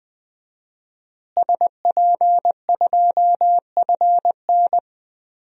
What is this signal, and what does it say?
Morse code, heard as a keyed tone, about 20 wpm, 700 Hz. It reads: SP2FN
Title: DXCC-100 Call Signs - 5 Characters - Repeated 2x - 20wpm